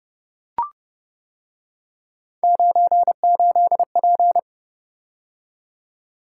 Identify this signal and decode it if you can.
Morse code, heard as a keyed tone, about 30 wpm, 700 Hz. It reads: 98P